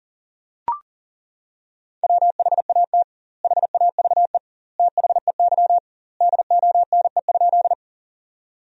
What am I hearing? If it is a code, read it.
Morse code, heard as a keyed tone, about 40 wpm, 700 Hz. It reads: WHAT HAVE THEY DONE?
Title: 2x Word-Spacing - Sentences from Top 500 Words - 40wpm